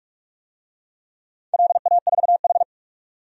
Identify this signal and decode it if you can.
Morse code, heard as a keyed tone, about 44 wpm, 690 Hz. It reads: LA4H
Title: DXCC-100 Call Signs - 4 Characters - 45wpm